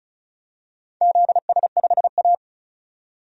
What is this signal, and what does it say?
Morse code, heard as a keyed tone, about 36 wpm, 700 Hz. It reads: ZS5A
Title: DXCC-100 Call Signs - 4 Characters - 35wpm